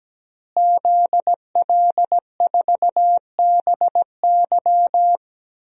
Morse code, heard as a keyed tone, about 17 wpm, 700 Hz. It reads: ZL4BY
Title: DXCC-100 Call Signs - 5 Characters - Repeated 2x - 17wpm